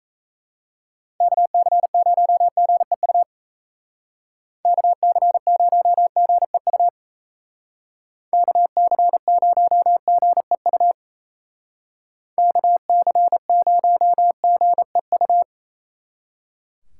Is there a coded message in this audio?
Yes: Morse code from 1.2 to 15.4 s, heard as a keyed tone, about 35 wpm, 700 Hz. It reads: KC0GEU KC0GEU KC0GEU KC0GEU